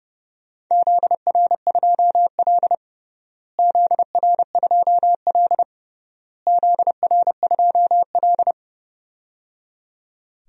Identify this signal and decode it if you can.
Morse code, heard as a keyed tone, about 30 wpm, 700 Hz. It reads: ZR2L ZR2L ZR2L
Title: DXCC-100 Call Signs - 4 Characters - Repeated 3x - 30wpm